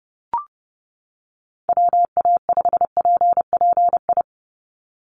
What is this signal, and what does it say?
Morse code, heard as a keyed tone, about 30 wpm, 700 Hz. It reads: WA5PPI
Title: USA Call Signs 30wpm